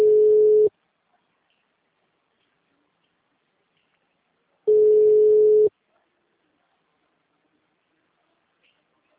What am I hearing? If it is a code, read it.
Ringback tone (Europe, 425 Hz).
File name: out- -2007- - - .15615